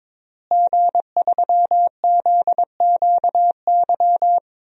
Morse code, heard as a keyed tone, about 22 wpm, 700 Hz. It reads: G3ZQY
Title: DXCC-100 Call Signs - 5 Characters - Repeated 3x - 22wpm